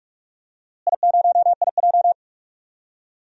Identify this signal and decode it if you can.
Morse code, heard as a keyed tone, about 46 wpm, 700 Hz. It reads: I0IJ
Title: DXCC-100 Call Signs - 4 Characters - Repeated 2x - 45wpm